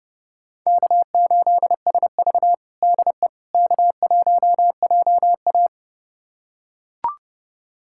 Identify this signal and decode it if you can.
Morse code, heard as a keyed tone, about 30 wpm, 700 Hz. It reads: K8SV DE K1JA
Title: USA Callsign DE Callsign - 4 Characters - 30wpm